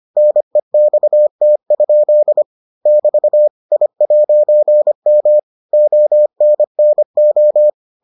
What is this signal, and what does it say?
Morse code, heard as a keyed tone, about 25 wpm, 600 Hz. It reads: NEXT? <BT> I'M ONNO